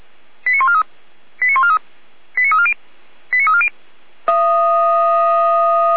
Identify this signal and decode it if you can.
Five-tone selective call (ZVEI), heard as quick runs of five tones, tones of about 70 ms. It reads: 89143 89143 89340 89340